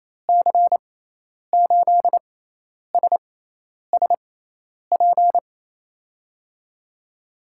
Morse code, heard as a keyed tone, about 28 wpm, spaced as for 10 wpm, 700 Hz. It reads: C8SSP
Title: USA Call Signs - 6 Characters - 6x Character Spacing - 28wpm